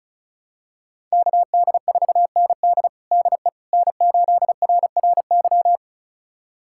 Morse code, heard as a keyed tone, about 36 wpm, 700 Hz. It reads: KD4ND DE N8RRY
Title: USA Callsign DE Callsign - 5 Characters - 35wpm